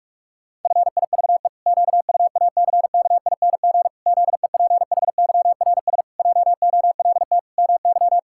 Morse code, heard as a keyed tone, about 46 wpm, 700 Hz. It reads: FIVE QUACKING ZEPHYRS JOLT MY
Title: Pangram Challenge v2 - 45wpm